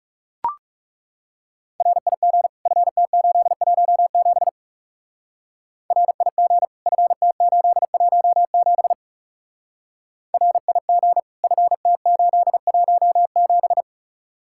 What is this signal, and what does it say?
Morse code, heard as a keyed tone, about 36 wpm, 700 Hz. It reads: RIG FT817 RIG FT817 RIG FT817
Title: Speed-Racing Common QSO Phrases Encore - 30wpm